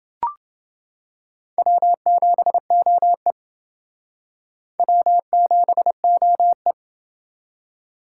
Morse code, heard as a keyed tone, about 29 wpm, 700 Hz. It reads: W7OE W7OE